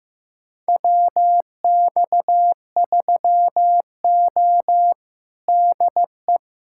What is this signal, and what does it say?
Morse code, heard as a keyed tone, about 15 wpm, 700 Hz. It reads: WX3O DE
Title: USA Callsign DE Callsign - 4 Characters - 15wpm